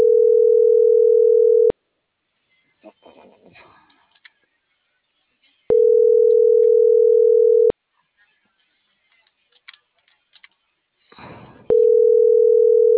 Ringback tone (North America), 440 + 480 Hz.